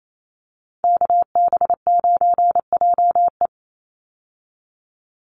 Morse code, heard as a keyed tone, about 28 wpm, 700 Hz. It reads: KB9JE
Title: USA Call Signs 28wpm